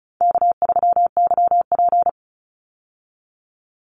Morse code, heard as a keyed tone, about 36 wpm, 700 Hz. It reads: K3YP
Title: Call Signs - 35wpm